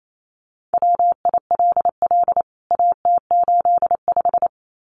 Morse code, heard as a keyed tone, about 28 wpm, 700 Hz. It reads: WILL AT85
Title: CWT Exchange 28wpm